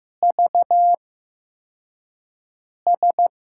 Morse code, heard as a keyed tone, about 15 wpm, spaced as for 4 wpm, 700 Hz. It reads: VS
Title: USA Call Signs - 6 Characters - 8x Character Spacing - 15wpm